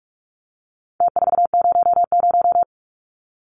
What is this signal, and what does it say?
Morse code, heard as a keyed tone, about 44 wpm, 700 Hz. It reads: T400
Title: Numbers - Contest Style - 4-Digit 45wpm